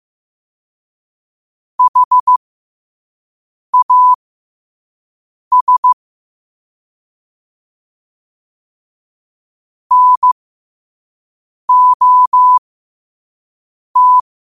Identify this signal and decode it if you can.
Morse code, heard as a keyed tone, about 15 wpm, spaced as for 5 wpm, 1000 Hz. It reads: HAS NOT
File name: qotdfiles / 5wpm